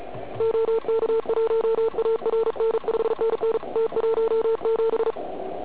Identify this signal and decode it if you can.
Morse code, heard as a keyed tone, about 35 wpm, 440 Hz, with noise. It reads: OK1ARN 5NN T17
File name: Archiv / CUC106 / cuc106 PILEUP